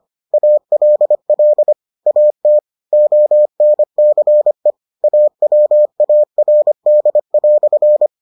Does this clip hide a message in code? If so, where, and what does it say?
Yes: Morse code from 0.3 to 8.1 s, heard as a keyed tone, about 25 wpm, 600 Hz. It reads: ALL AT ONCE AWARD"